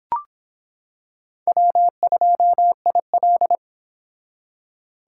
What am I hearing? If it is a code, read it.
Morse code, heard as a keyed tone, about 26 wpm, 700 Hz. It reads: W2IL